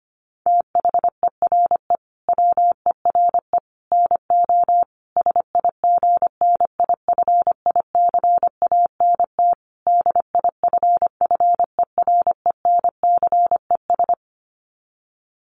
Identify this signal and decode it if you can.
Morse code, heard as a keyed tone, about 25 wpm, 700 Hz. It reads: THERE WERE NO SIGNIFICANT DIFFERENCES
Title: Sets of 5 Words Encore2 25wpm